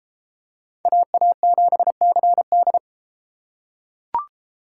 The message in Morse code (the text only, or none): AA7CD